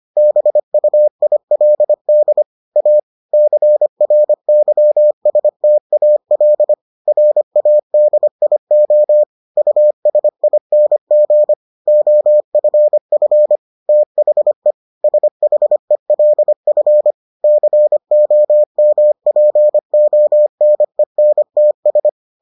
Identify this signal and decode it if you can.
Morse code, heard as a keyed tone, about 25 wpm, 600 Hz. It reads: BUILD A CRYSTAL RADIO USING OFF THE SHELF COMPONENTS